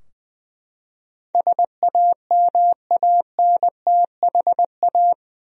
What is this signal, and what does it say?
Morse code, heard as a keyed tone, about 20 wpm, 700 Hz. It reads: SAMANTHA
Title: USA Common Names - 20wpm